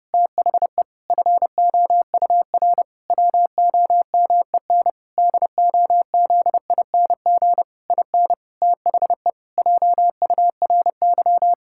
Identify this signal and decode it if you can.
Morse code, heard as a keyed tone, about 30 wpm, 700 Hz. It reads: THE FOUR WOMEN DOZING IN THE JURY
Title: Pangram Challenge v2 - 30wpm